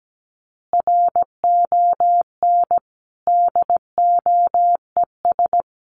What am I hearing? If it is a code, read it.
Morse code, heard as a keyed tone, about 17 wpm, 700 Hz. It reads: RON DOES
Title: Additional Easy Phrases 17wpm